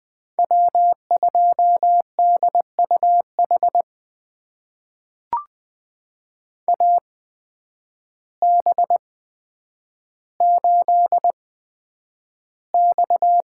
Morse code, heard as a keyed tone, about 20 wpm, spaced as for 9 wpm, 700 Hz. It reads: W2DUH AB8X
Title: USA Call Signs - 5 Characters - 8x Character Spacing - 20wpm